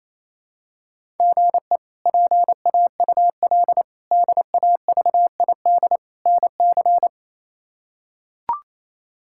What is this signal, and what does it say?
Morse code, heard as a keyed tone, about 28 wpm, 700 Hz. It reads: GE PAUL DAVID NC
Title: K1USN SST QSO Practice - 28wpm